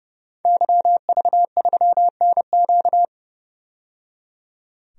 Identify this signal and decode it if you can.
Morse code, heard as a keyed tone, about 30 wpm, 700 Hz. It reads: YV3NQ
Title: DXCC-100 Call Signs - 5 Characters - 30wpm